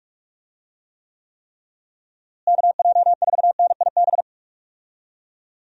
Morse code, heard as a keyed tone, about 45 wpm, 700 Hz. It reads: KJ4NIB
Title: USA Call Signs - 6 Characters - Repeated 3x - 45wpm